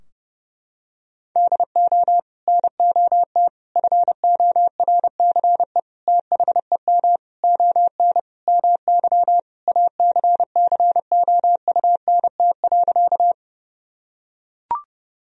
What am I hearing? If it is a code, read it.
Morse code, heard as a keyed tone, about 30 wpm, 700 Hz. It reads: DO NOT FORCE THEM ON MY ACCOUNT.
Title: Sentences from Top 1000 Words - Review Part 3 - 30wpm